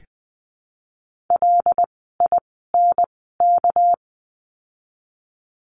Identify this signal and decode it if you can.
Morse code, heard as a keyed tone, about 20 wpm, spaced as for 14 wpm, 700 Hz. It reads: LINK
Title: 199 Lesson 32f - Letter k - Words 20wpm